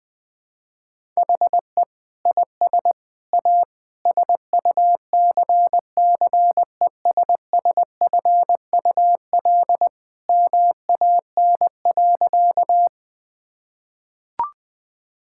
Morse code, heard as a keyed tone, about 21 wpm, 700 Hz. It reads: HE IS A SUCCESSFUL MAN.
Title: Sentences from Top 1000 Words - Review Part 6 - 20wpm